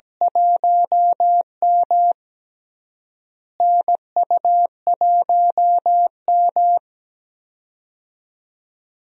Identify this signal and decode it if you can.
Morse code, heard as a keyed tone, about 17 wpm, 700 Hz. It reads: 1M NU1M
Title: USA Call Signs - 4 Characters - Repeated 3x - 17wpm